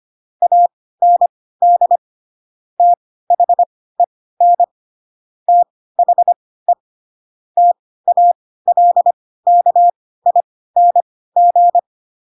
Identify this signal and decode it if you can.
Morse code, heard as a keyed tone, about 25 wpm, spaced as for 16 wpm, 700 Hz. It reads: AND THEN THE TALKING
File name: Pinocchio / Pinocchio ch7 10 25 ik1hge